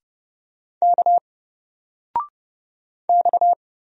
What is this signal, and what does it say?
Morse code, heard as a keyed tone, about 31 wpm, 700 Hz. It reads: K X
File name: Single-Letter-Number-30wpm